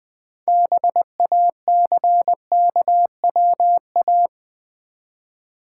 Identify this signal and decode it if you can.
Morse code, heard as a keyed tone, about 20 wpm, 700 Hz. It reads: BACKWA
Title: Build A Word - 3 Letters At A Time - 9-Letter Words - 20wpm